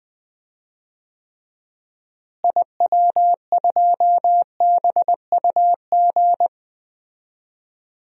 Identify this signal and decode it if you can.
Morse code, heard as a keyed tone, about 21 wpm, 700 Hz. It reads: IW2BUG
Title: DXCC-100 Call Signs - 6 Characters - Repeated 2x - 20wpm